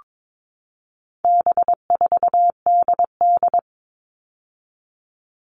Morse code, heard as a keyed tone, about 22 wpm, 700 Hz. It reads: B4DD
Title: All Cut Numbers - 4-Digit 22wpm